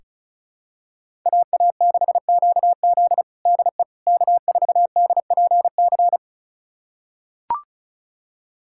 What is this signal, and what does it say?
Morse code, heard as a keyed tone, about 34 wpm, 700 Hz. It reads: AA6QZ DE K4DPC
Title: USA Callsign DE Callsign - 5 Characters - 35wpm